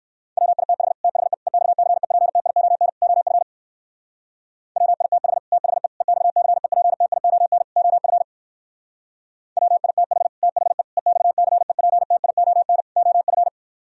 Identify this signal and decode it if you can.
Morse code, heard as a keyed tone, about 58 wpm, 700 Hz. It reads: WITH THE EXCEPTION OF WITH THE EXCEPTION OF WITH THE EXCEPTION OF